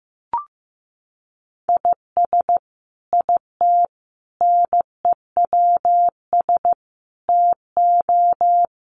Morse code, heard as a keyed tone, about 15 wpm, 700 Hz. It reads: IS IT NEWS TO
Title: Sentences from Top 800 Words 15wpm